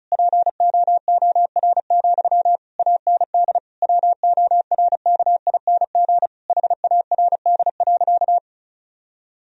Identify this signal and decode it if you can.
Morse code, heard as a keyed tone, about 36 wpm, 700 Hz. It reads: POOR, AND WORKING HARD.
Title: Sentences from Top 1000 Words - Review Part 4 - 35wpm